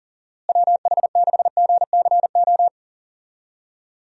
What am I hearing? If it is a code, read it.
Morse code, heard as a keyed tone, about 40 wpm, 700 Hz. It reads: WH6GCO
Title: USA Call Signs - 6 Characters - Repeated 3x - 40wpm